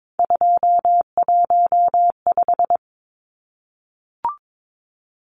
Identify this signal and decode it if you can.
Morse code, heard as a keyed tone, about 22 wpm, 700 Hz. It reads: 215
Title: Numbers - 3-Digit 22wpm